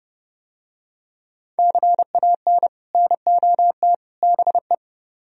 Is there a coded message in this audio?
Yes: Morse code from 1.6 to 4.7 s, heard as a keyed tone, about 30 wpm, 700 Hz. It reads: CAN NOT BE